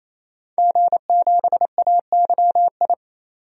Morse code, heard as a keyed tone, about 28 wpm, 700 Hz. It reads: G7AYI
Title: DXCC-100 Call Signs - 5 Characters - 28wpm